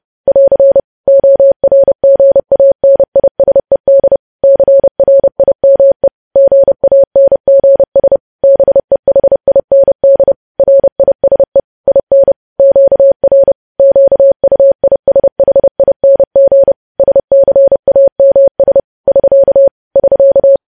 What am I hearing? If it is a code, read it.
Morse code, heard as a keyed tone, about 30 wpm, 550 Hz. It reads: <AR> ORGANISED CRIME GANGS BEHIND RISE IN QR QUISHING SCAMS <SK> <SK>
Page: news headlines in morse code at 30 wpm